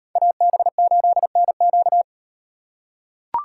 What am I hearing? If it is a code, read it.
Morse code, heard as a keyed tone, about 38 wpm, 700 Hz. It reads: AB8NQ